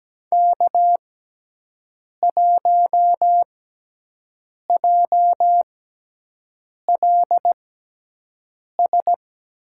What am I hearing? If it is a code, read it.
Morse code, heard as a keyed tone, about 17 wpm, spaced as for 6 wpm, 700 Hz. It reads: K1JLS
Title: USA Call Signs - 5 Characters - 6x Character Spacing - 17wpm